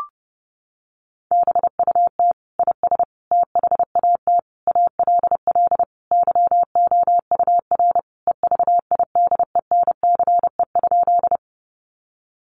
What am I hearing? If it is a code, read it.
Morse code, heard as a keyed tone, about 30 wpm, 700 Hz. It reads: BUT IS THAT ALL YOUR EVIDENCE?
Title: Sentences from Top 1300 Words 30wpm